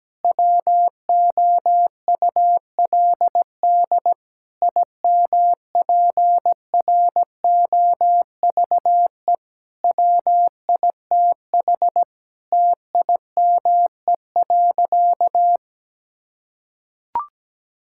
Morse code, heard as a keyed tone, about 17 wpm, 700 Hz. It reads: WOULD IMPROVE WITH TIME.
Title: Sentences from Top 1000 Words - Review Part 5 - 17wpm